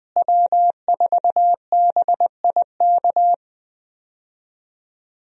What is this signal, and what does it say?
Morse code, heard as a keyed tone, about 20 wpm, 700 Hz. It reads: W4BIK
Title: USA Call Signs - 5 Characters - Repeated 2x - 20wpm